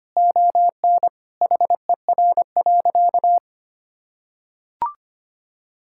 Morse code, heard as a keyed tone, about 25 wpm, 700 Hz. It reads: ON HER.
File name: practice / 25wpm / Sentences with 2-Word Phrases - 25wpm